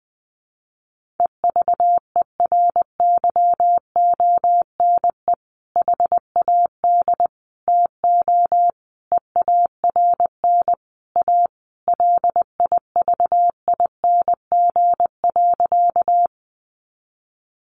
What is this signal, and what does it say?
Morse code, heard as a keyed tone, about 20 wpm, 700 Hz. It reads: EVERYONE HAD TO EARN A LIVING.
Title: Sentences from Top 1500 Words 20wpm